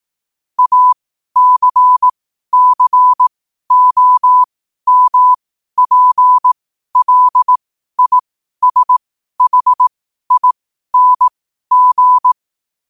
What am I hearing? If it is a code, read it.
Morse code, heard as a keyed tone, about 18 wpm, spaced as for 12 wpm, 1000 Hz. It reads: ACCOMPLISHING
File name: qotdfiles / 12wpm